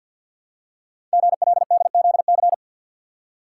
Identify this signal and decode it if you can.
Morse code, heard as a keyed tone, about 50 wpm, 700 Hz. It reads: GLDZC
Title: Code Groups - 5 Letters - 50wpm